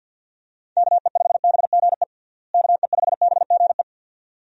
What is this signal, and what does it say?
Morse code, heard as a keyed tone, about 51 wpm, 700 Hz. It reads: KE5BGE KE5BGE
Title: USA Call Signs - 6 Characters - Repeated 2x - 50wpm